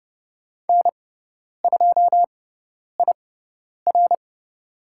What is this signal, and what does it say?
Morse code, heard as a keyed tone, about 30 wpm, spaced as for 10 wpm, 700 Hz. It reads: N2IR
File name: Lesson 16g - Letter c - Call Signs 30wpm